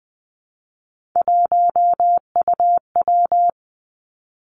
Morse code, heard as a keyed tone, about 20 wpm, 700 Hz. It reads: 1UW
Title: Three Challenging Letter-Number Combinations 20wpm